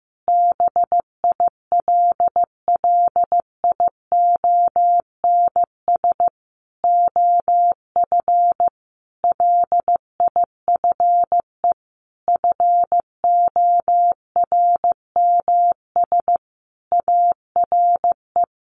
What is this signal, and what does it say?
Morse code, heard as a keyed tone, about 15 wpm, 700 Hz. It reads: BILLIONS OF LIFE FORMS ARE
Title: A Sound Of Thunder 15wpm